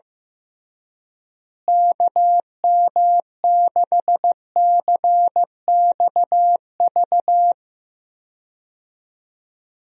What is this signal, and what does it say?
Morse code, heard as a keyed tone, about 15 wpm, 700 Hz. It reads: KM6CXV
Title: USA Call Signs - 6 Characters - Repeated 3x - 15wpm